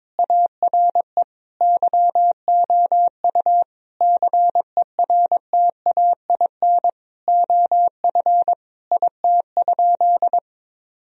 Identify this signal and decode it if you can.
Morse code, heard as a keyed tone, about 22 wpm, 700 Hz. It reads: ARE YOU CERTAIN OF IT?
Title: Sentences from Top 1000 Words - Review Part 3 - 22wpm